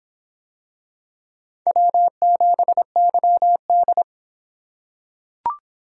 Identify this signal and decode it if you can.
Morse code, heard as a keyed tone, about 23 wpm, 700 Hz. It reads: W7YD